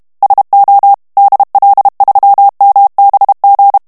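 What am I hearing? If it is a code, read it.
Morse code, heard as a keyed tone, about 32 wpm, 800 Hz. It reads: SO DL3MBG
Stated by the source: German plain-text Morse reading practice at 30 words per minute.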